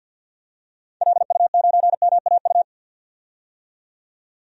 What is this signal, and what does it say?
Morse code, heard as a keyed tone, about 52 wpm, 700 Hz. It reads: LU9MAU
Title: DXCC-100 Call Signs - 6 Characters - 50wpm